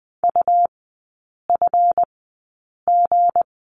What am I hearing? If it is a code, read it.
Morse code, heard as a keyed tone, about 20 wpm, spaced as for 8 wpm, 700 Hz. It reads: UFG
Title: 198 Lesson 32e - Letter k - Three Characters 20wpm